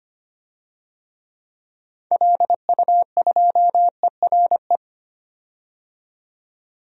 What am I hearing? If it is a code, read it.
Morse code, heard as a keyed tone, about 25 wpm, 700 Hz. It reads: LU2ERE
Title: DXCC-100 Call Signs - 6 Characters - 25wpm